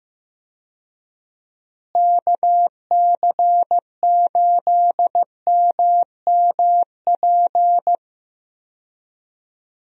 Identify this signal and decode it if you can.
Morse code, heard as a keyed tone, about 15 wpm, 700 Hz. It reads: KC8MMP
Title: USA Call Signs - 6 Characters - Repeated 3x - 15wpm